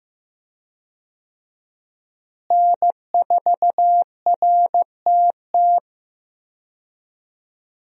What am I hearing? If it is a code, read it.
Morse code, heard as a keyed tone, about 15 wpm, 700 Hz. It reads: N4RTT
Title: USA Call Signs - 5 Characters - Repeated 2x - 15wpm